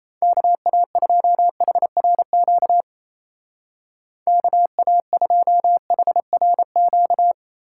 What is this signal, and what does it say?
Morse code, heard as a keyed tone, about 33 wpm, 700 Hz. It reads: KA2HRQ KA2HRQ
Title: Speed Racing - USA 2x3 Call Signs 28wpm